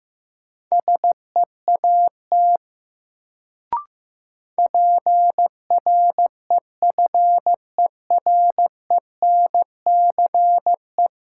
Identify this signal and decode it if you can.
Morse code, heard as a keyed tone, about 15 wpm, 700 Hz. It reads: SEAT PREFERENCE
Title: Sentences from Top 1500 Words - New Word Drill - 15wpm